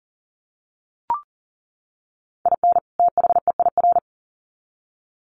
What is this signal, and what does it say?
Morse code, heard as a keyed tone, about 40 wpm, 700 Hz. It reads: IN THEIR
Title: Sets of 2 Words 40wpm